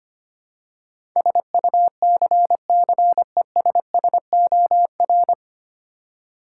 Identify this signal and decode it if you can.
Morse code, heard as a keyed tone, about 25 wpm, 700 Hz. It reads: SUCCESSOR
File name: practice / 25wpm / 9 Letter Words - First 3 Letters Given - 25wpm